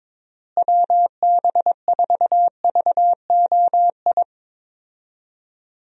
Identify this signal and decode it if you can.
Morse code, heard as a keyed tone, about 22 wpm, 700 Hz. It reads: WB4VOI
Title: USA Call Signs - 6 Characters - Repeated 3x - 22wpm